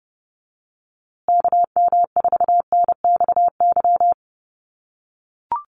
Morse code, heard as a keyed tone, about 30 wpm, 700 Hz. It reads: KM4NXY
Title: USA Call Signs Repeated 2x 30wpm